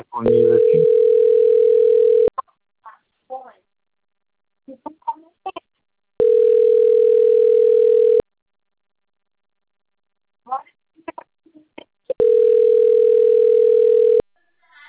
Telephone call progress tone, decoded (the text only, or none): ringback tone